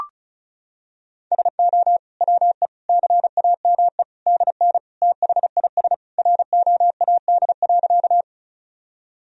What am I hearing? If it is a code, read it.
Morse code, heard as a keyed tone, about 36 wpm, 700 Hz. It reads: SO WE CAME DN THIS ROAD.
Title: Sentences from Top 400 Words and Abbreviations 35wpm